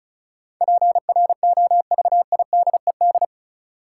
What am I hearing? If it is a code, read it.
Morse code, heard as a keyed tone, about 36 wpm, 700 Hz. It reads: PROVIDED
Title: Sentences from Top 1500 Words - New Word Drill - 35wpm